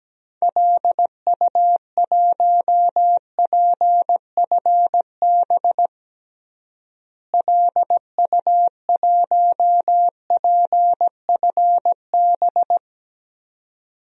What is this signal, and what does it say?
Morse code, heard as a keyed tone, about 17 wpm, 700 Hz. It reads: LU1PFB LU1PFB
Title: DXCC-100 Call Signs - 6 Characters - Repeated 2x - 17wpm